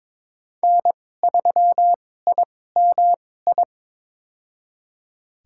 Morse code, heard as a keyed tone, about 22 wpm, spaced as for 16 wpm, 700 Hz. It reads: N3IMI
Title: USA Call Signs - 5 Characters - 2x Character Spacing - 22wpm